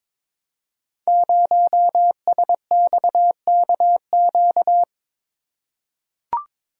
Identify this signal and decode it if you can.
Morse code, heard as a keyed tone, about 22 wpm, 700 Hz. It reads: 0SXKQ
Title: Code Groups - 5 Letter:Numbers - 22wpm